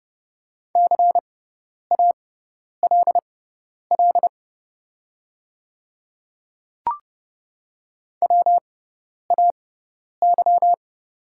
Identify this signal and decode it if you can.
Morse code, heard as a keyed tone, about 30 wpm, spaced as for 10 wpm, 700 Hz. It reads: CALL WAY
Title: 6x Character Spacing - Top 100 Words - 30wpm